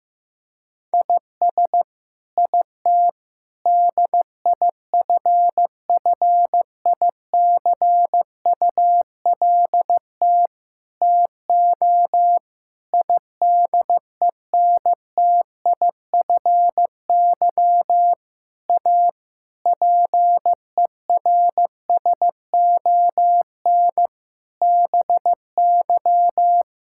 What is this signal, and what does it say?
Morse code, heard as a keyed tone, about 15 wpm, 700 Hz. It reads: IS IT DIFFICULT TO IDENTIFY A PERSON BY
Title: Sentences from Top 1000 Words - Review Part 6 - 15wpm